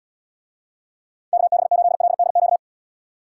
Morse code, heard as a keyed tone, about 71 wpm, 700 Hz. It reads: KB8LDQ